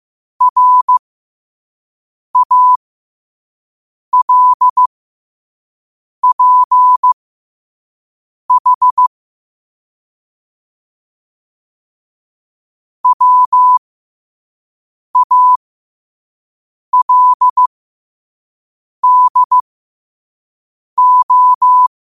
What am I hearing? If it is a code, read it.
Morse code, heard as a keyed tone, about 15 wpm, spaced as for 5 wpm, 1000 Hz. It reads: RALPH WALDO
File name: qotdfiles / 5wpm